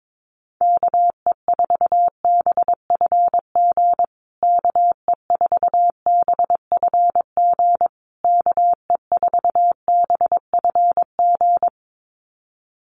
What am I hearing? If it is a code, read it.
Morse code, heard as a keyed tone, about 22 wpm, 700 Hz. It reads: KE4BFG KE4BFG KE4BFG
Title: USA Call Signs Repeated 3x 22wpm